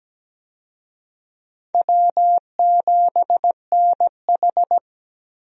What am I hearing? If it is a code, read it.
Morse code, heard as a keyed tone, about 17 wpm, 700 Hz. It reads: W7NH
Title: USA Call Signs - 4 Characters - Repeated 3x - 17wpm